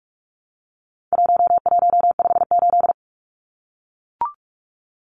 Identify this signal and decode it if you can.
Morse code, heard as a keyed tone, about 44 wpm, 700 Hz. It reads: 1158
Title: Numbers - 4-Digit 45wpm